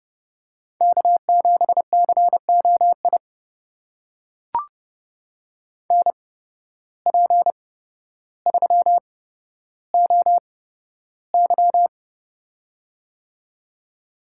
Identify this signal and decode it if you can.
Morse code, heard as a keyed tone, about 30 wpm, spaced as for 13 wpm, 700 Hz. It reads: K7COI NP3OY
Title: USA Call Signs - 8x Character Spacing - 30wpm